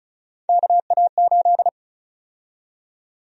Morse code, heard as a keyed tone, about 34 wpm, 700 Hz. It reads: KA8
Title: USA Call Sign Prefixes - Repeated 2x - 35wpm